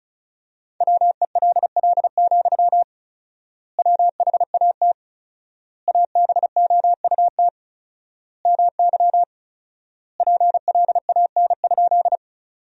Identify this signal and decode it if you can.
Morse code, heard as a keyed tone, about 34 wpm, 700 Hz. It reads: WELL, WHAT ABOUT MY PLAN?
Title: 4x Word-Spacing - Sentences from Top 500 Words - 35wpm